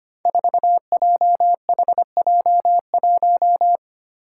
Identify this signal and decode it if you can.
Morse code, heard as a keyed tone, about 25 wpm, 700 Hz. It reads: 4JHJ1
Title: Five Challenging Letter-Number Combinations 25wpm